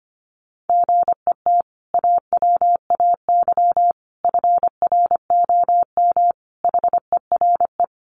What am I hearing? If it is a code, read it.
Morse code, heard as a keyed tone, about 25 wpm, 700 Hz. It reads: GET AWAY FROM HERE
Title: Sentences from Top 600 Words 25wpm